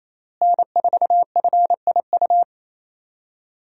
Morse code, heard as a keyed tone, about 28 wpm, 700 Hz. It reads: N4FIU
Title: USA Call Signs - 5 Characters - Repeated 2x - 28wpm